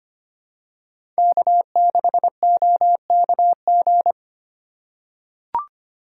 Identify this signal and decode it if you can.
Morse code, heard as a keyed tone, about 25 wpm, 700 Hz. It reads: K6OKG
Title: USA Call Signs - 5 Characters - 2x Character Spacing - 25wpm